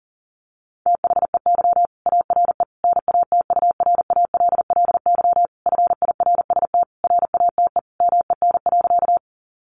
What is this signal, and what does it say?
Morse code, heard as a keyed tone, about 40 wpm, 700 Hz. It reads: THEY ARE NATURALLY FIRST RATE MEN.
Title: Sentences from Top 1300 Words 40wpm